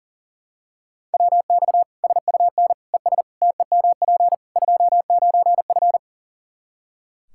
Morse code, heard as a keyed tone, about 37 wpm, 700 Hz. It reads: WX SUN ES TEMP 29F
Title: Practice QSO - Exchange 2 - Info - 40wpm